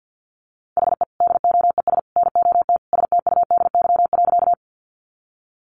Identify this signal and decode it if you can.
Morse code, heard as a keyed tone, about 48 wpm, 700 Hz. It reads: HE DOES NOT STUDY.
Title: Additional Easy Phrases 50wpm